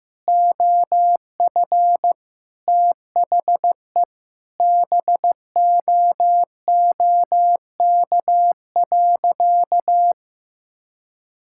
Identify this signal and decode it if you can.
Morse code, heard as a keyed tone, about 16 wpm, 700 Hz. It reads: OF THE BOOK.
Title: Sentences from Top 1000 Words - Review Part 5 - 15wpm